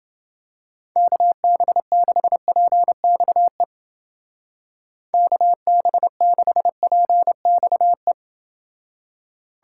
Morse code, heard as a keyed tone, about 29 wpm, 700 Hz. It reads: KB6PXE KB6PXE